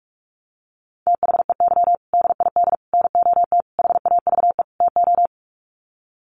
Morse code, heard as a keyed tone, about 44 wpm, 700 Hz. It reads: THEY DID NOT HAVE TO
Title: Sets of 5 Words Encore2 45wpm